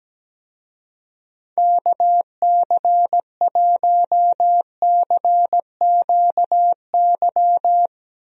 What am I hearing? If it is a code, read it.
Morse code, heard as a keyed tone, about 17 wpm, 700 Hz. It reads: KC1CQY
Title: USA Call Signs - 6 Characters - 17wpm